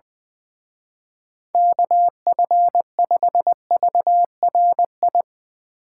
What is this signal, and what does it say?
Morse code, heard as a keyed tone, about 22 wpm, 700 Hz. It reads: KF5VRI
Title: Speed Racing - USA 2x3 Call Signs 20wpm